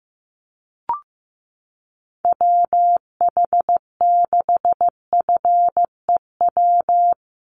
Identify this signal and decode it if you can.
Morse code, heard as a keyed tone, about 15 wpm, 700 Hz. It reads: WH6FEW
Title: USA Call Signs Repeated 3x 15wpm